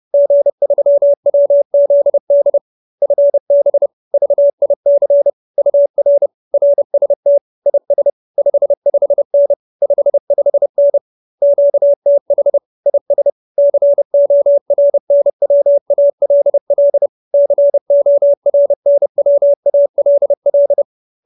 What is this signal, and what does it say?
Morse code, heard as a keyed tone, about 30 wpm, 570 Hz. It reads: G3WZD FB VIC UR RST IS 55N 55N QTH IS CORNWALL CORNWALL
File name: QSO310 30